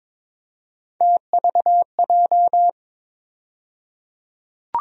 Morse code, heard as a keyed tone, about 22 wpm, 700 Hz. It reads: TVJ
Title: USA Call Sign Postfixes - 3 Letters 22wpm